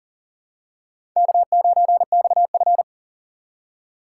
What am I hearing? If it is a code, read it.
Morse code, heard as a keyed tone, about 40 wpm, 700 Hz. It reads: K9XF
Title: USA Call Signs - 6x Character Spacing - 40wpm